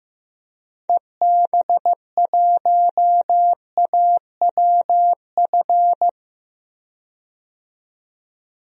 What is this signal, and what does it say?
Morse code, heard as a keyed tone, about 15 wpm, 700 Hz. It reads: EB1AWF
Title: DXCC-100 Call Signs - 6 Characters - 15wpm